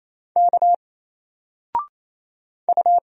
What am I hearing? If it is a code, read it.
Morse code, heard as a keyed tone, about 27 wpm, 700 Hz. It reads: K U
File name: Single-Letter-Number-28wpm